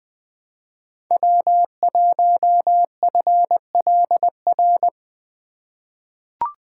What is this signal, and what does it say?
Morse code, heard as a keyed tone, about 20 wpm, 700 Hz. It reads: W1FLR
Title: USA Call Signs - 5 Characters - 4x Character Spacing - 20wpm